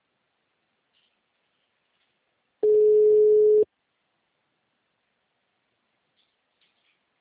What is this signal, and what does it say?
Ringback tone (Europe, 425 Hz).
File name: out- -2023- - - .55833